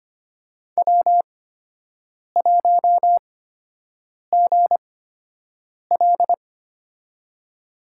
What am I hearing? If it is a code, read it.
Morse code, heard as a keyed tone, about 25 wpm, spaced as for 7 wpm, 700 Hz. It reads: W1GL
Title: USA Call Signs - 4 Characters - 8x Character Spacing - 25wpm